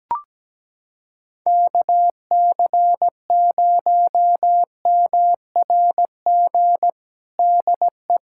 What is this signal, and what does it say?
Morse code, heard as a keyed tone, about 17 wpm, 700 Hz. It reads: KC0MRG DE
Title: USA Callsign DE Callsign - 6 Characters - 17wpm